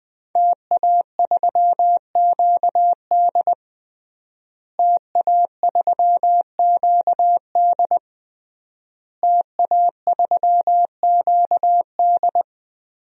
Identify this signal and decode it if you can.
Morse code, heard as a keyed tone, about 21 wpm, 700 Hz. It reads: TA3QD TA3QD TA3QD
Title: DXCC-100 Call Signs - 5 Characters - Repeated 3x - 20wpm